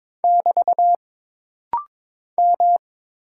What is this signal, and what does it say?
Morse code, heard as a keyed tone, about 22 wpm, 700 Hz. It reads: <BT> M
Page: This rapid-fire timing is geared to push your ICR (Instant Character Recognition) skills.